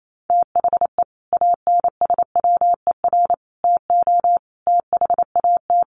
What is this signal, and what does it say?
Morse code, heard as a keyed tone, about 28 wpm, 700 Hz. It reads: THE ANSWER TO THAT
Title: Sets of 5 Words Encore2 28wpm